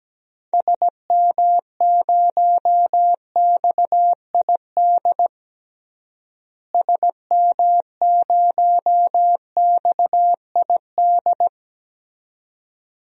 Morse code, heard as a keyed tone, about 17 wpm, 700 Hz. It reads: SM0XID SM0XID
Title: DXCC-100 Call Signs - 6 Characters - Repeated 2x - 17wpm